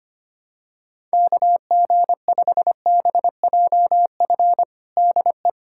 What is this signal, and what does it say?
Morse code, heard as a keyed tone, about 25 wpm, 700 Hz. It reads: KG5BJF DE
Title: USA Callsign DE Callsign - 6 Characters - 25wpm